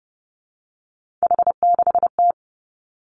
Morse code, heard as a keyed tone, about 30 wpm, 700 Hz. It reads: H6T
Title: Three Challenging Letter-Number Combinations 30wpm